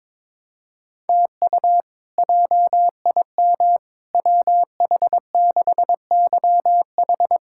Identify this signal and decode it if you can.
Morse code, heard as a keyed tone, about 22 wpm, 700 Hz. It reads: TU JIM WH6YH
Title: K1USN SST QSO Practice - 22wpm